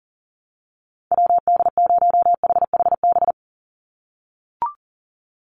Morse code, heard as a keyed tone, about 40 wpm, 700 Hz. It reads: WD0HHB
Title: USA Call Signs 40wpm